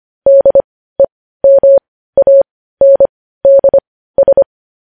Morse code, heard as a keyed tone, about 25 wpm, spaced as for 15 wpm, 550 Hz. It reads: DEMANDS
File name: podcast / News headlines-1x-25x15